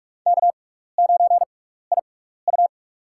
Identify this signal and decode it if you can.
Morse code, heard as a keyed tone, about 46 wpm, spaced as for 16 wpm, 700 Hz. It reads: K9IU
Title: USA Call Signs - 4 Characters - 6x Character Spacing - 45wpm